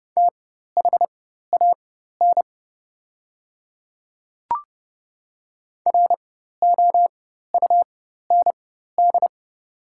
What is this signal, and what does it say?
Morse code, heard as a keyed tone, about 30 wpm, spaced as for 14 wpm, 700 Hz. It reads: THAN ROUND
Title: 4x Character Spacing - Top 200 Words - 30wpm